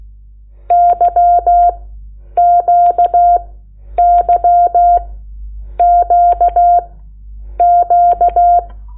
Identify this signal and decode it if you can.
Morse code, heard as a keyed tone, about 16 wpm, spaced as for 9 wpm, 700 Hz. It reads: YQYQQ
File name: Lecciones / CLASE Nº8